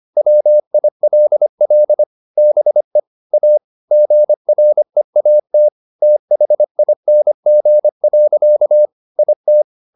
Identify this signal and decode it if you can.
Morse code, heard as a keyed tone, about 25 wpm, 600 Hz. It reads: WILL BE A GREAT THING. IT